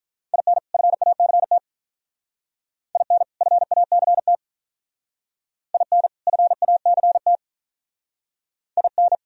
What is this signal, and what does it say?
Morse code, heard as a keyed tone, about 44 wpm, 700 Hz. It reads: IN FACT IN FACT IN FACT IN